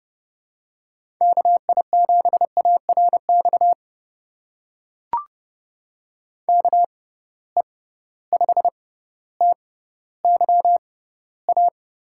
Morse code, heard as a keyed tone, about 32 wpm, spaced as for 16 wpm, 700 Hz. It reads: KI7ARX KE5TYA
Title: USA Call Signs - 6 Characters - 6x Character Spacing - 30wpm